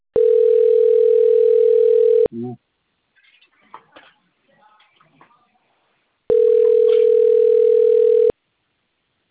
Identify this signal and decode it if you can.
Ringback tone (North America), 440 + 480 Hz, with speech.
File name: out- - - - - .15234